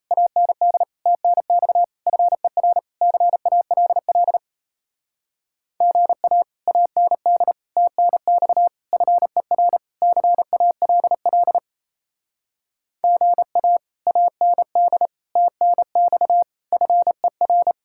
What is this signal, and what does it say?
Morse code, heard as a keyed tone, about 34 wpm, 700 Hz. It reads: AND TNX FER CALL GA AND TNX FER CALL GA AND TNX FER
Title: Speed-Racing Common QSO Phrases - 28wpm